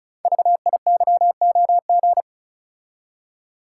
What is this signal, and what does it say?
Morse code, heard as a keyed tone, about 36 wpm, 700 Hz. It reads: VIYOG
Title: Code Groups - 5 Letters - 35wpm